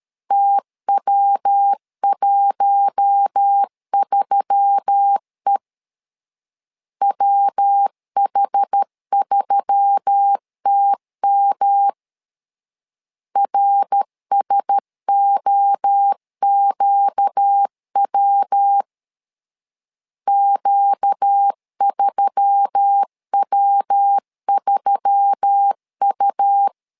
Morse code, heard as a keyed tone, about 13 wpm, 800 Hz. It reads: TW13E WH3TM RSOQW Q3W3U
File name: LEZIONI / 4 Q1S3 12wpm 2023